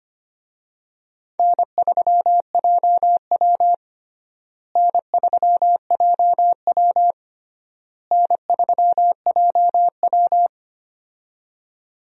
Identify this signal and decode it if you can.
Morse code, heard as a keyed tone, about 25 wpm, 700 Hz. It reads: N3JW N3JW N3JW
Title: USA Call Signs - 4 Characters - Repeated 3x - 25wpm